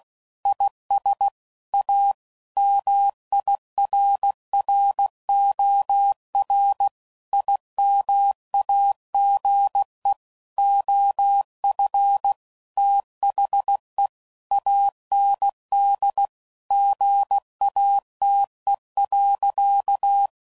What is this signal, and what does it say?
Morse code, heard as a keyed tone, about 16 wpm, 800 Hz. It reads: IS A MIRROR IMAGE OF THE AND GATE.
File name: CW Code Groups @15WPM / Technical Subjects / OR Gate